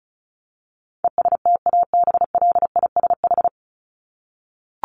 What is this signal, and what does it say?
Morse code, heard as a keyed tone, about 34 wpm, 700 Hz. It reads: ESTABLISH
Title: ED Suffix - Top 100 Words with Base Words 35wpm